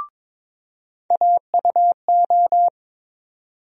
Morse code, heard as a keyed tone, about 22 wpm, 700 Hz. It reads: AUO
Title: USA Call Sign Postfixes - 3 Letters 22wpm